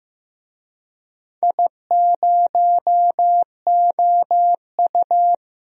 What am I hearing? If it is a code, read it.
Morse code, heard as a keyed tone, about 16 wpm, 700 Hz. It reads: I0OU
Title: DXCC-100 Call Signs - 4 Characters - Repeated 2x - 15wpm